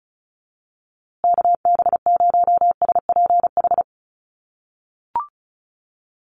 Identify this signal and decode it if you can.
Morse code, heard as a keyed tone, about 34 wpm, 700 Hz. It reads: KB0SPH
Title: USA Call Signs Repeated 2x 35wpm